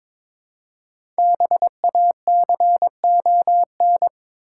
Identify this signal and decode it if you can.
Morse code, heard as a keyed tone, about 22 wpm, 700 Hz. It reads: BACON
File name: Difficult Words with Hints - 22wpm